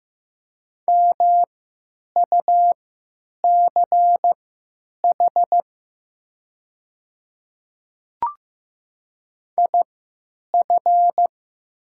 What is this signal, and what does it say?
Morse code, heard as a keyed tone, about 15 wpm, spaced as for 9 wpm, 700 Hz. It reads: MUCH IF
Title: 3x Character Spacing - Top 200 Words - 15wpm